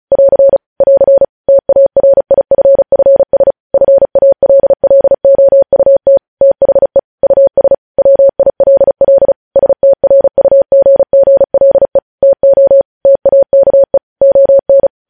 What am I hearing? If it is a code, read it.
Morse code, heard as a keyed tone, about 36 wpm, 550 Hz. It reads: <AR> <AR> TARIFFS FALLOUT THE US WILL STRUGGLE TO TAKE ON
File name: podcast / News headlines-1x-35x35